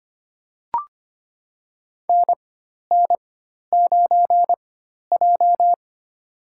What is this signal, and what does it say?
Morse code, heard as a keyed tone, about 25 wpm, spaced as for 12 wpm, 700 Hz. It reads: NN9J
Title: USA Call Signs - 4 Characters - 4x Character Spacing - 25wpm